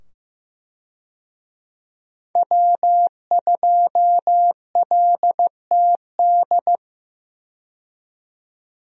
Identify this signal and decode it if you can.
Morse code, heard as a keyed tone, about 15 wpm, 700 Hz. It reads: W2LTD
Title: USA Call Signs - 5 Characters - 15wpm